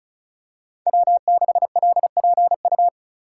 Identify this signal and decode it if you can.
Morse code, heard as a keyed tone, about 34 wpm, 700 Hz. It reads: W6LPU
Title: USA Call Signs - 2x Character Spacing - 35wpm